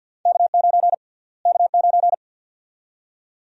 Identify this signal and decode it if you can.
Morse code, heard as a keyed tone, about 52 wpm, 700 Hz. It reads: K9 K9
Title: USA Call Sign Prefixes - Repeated 2x - 50wpm